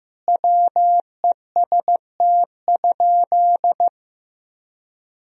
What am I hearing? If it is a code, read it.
Morse code, heard as a keyed tone, about 15 wpm, 700 Hz. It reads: WEST?
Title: Sentences from Top 1000 Words - Review Part 2 - 15wpm